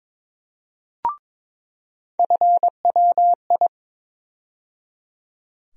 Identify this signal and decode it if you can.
Morse code, heard as a keyed tone, about 22 wpm, 700 Hz. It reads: FWI
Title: USA Call Sign Postfixes - 3 Letters 22wpm